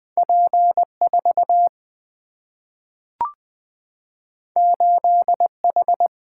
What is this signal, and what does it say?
Morse code, heard as a keyed tone, about 20 wpm, 700 Hz. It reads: P 4 8 H
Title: Single-Letter Single-Number 20wpm